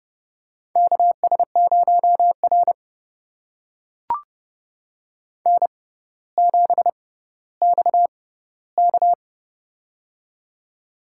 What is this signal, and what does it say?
Morse code, heard as a keyed tone, about 30 wpm, spaced as for 15 wpm, 700 Hz. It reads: KS0R N7XK
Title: USA Call Signs - 6x Character Spacing - 30wpm